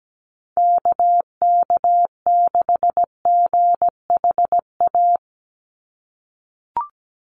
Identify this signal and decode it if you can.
Morse code, heard as a keyed tone, about 17 wpm, 700 Hz. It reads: KK6GHA
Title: Call Signs - 17wpm